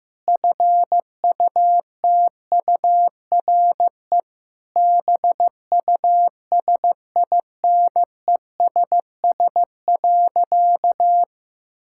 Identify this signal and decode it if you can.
Morse code, heard as a keyed tone, about 16 wpm, 700 Hz. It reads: FUTURE BUSINESS.
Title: Sentences from Top 1000 Words - Review Part 6 - 15wpm